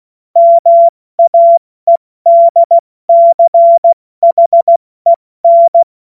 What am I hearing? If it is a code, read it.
Morse code, heard as a keyed tone, about 16 wpm, 680 Hz. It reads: MAEDCHEN
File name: Morsekurs / Witze / Witz 23. August 2022 70 F80